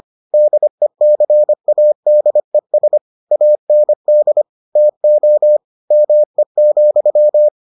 Morse code, heard as a keyed tone, about 25 wpm, 600 Hz. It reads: DECADES AND TO ME,